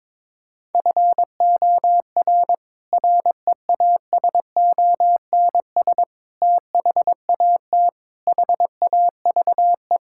Morse code, heard as a keyed tone, about 22 wpm, 700 Hz. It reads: FOR REASONS THAT HAVE
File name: practice / 22wpm / Sets of 8 Words - 22wpm